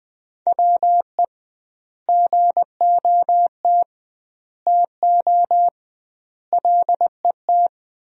Morse code, heard as a keyed tone, about 20 wpm, 700 Hz. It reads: WE GOT TO LET
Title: 2x Word-Spacing - Sentences from Top 300 Words - 20wpm